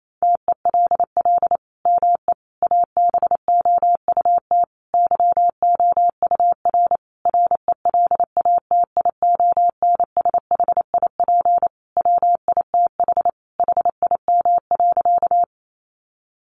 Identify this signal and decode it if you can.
Morse code, heard as a keyed tone, about 28 wpm, 700 Hz. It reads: TELL ME ABOUT YOUR RELATIONSHIP WITH HIM.
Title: Sentences from Top 1100 Words 28wpm